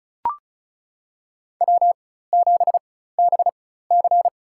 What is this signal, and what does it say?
Morse code, heard as a keyed tone, about 34 wpm, spaced as for 16 wpm, 700 Hz. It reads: W7BC
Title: USA Call Signs - 4x Character Spacing - 35wpm